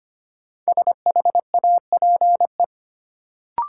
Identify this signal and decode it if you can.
Morse code, heard as a keyed tone, about 25 wpm, 700 Hz. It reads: SHAPE
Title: Sets of 5 Words Encore - New Word Drill - 25wpm